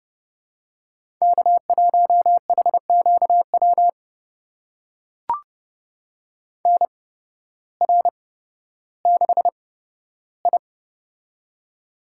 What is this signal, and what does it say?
Morse code, heard as a keyed tone, about 31 wpm, spaced as for 14 wpm, 700 Hz. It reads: K1HQW NR6I
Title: USA Call Signs - 8x Character Spacing - 30wpm